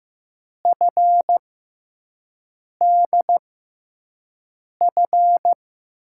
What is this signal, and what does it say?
Morse code, heard as a keyed tone, about 15 wpm, spaced as for 5 wpm, 700 Hz. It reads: FDF
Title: USA Call Signs - 6 Characters - 6x Character Spacing - 15wpm